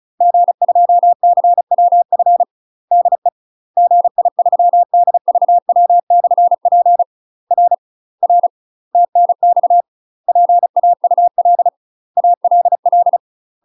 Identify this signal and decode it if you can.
Morse code, heard as a keyed tone, about 34 wpm, 700 Hz. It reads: G2CWF DE GI3DVW/P R R TNX PAUL ALL
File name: UK MASTER QSOs TRAINING / 35WPM / UK MASTER QSOs 0017